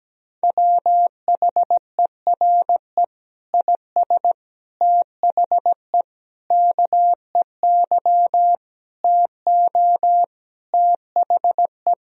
Morse code, heard as a keyed tone, about 17 wpm, 700 Hz. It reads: WHERE IS THE KEY TO THE
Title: Sentences from Top 1000 Words - Review Part 4 - 17wpm